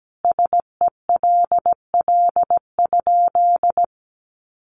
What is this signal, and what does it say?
Morse code, heard as a keyed tone, about 17 wpm, 700 Hz. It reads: SELL?
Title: Sentences from Top 1200 Words 17wpm